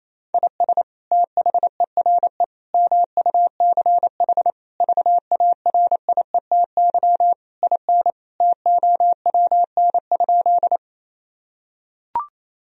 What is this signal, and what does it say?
Morse code, heard as a keyed tone, about 28 wpm, 700 Hz. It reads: IS THERE MUCH VARIETY IN TOWN?
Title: Sentences from Top 1000 Words - Review Part 4 - 28wpm